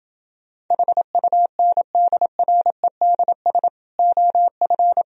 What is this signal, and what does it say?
Morse code, heard as a keyed tone, about 30 wpm, 700 Hz. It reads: HUNDREDS OF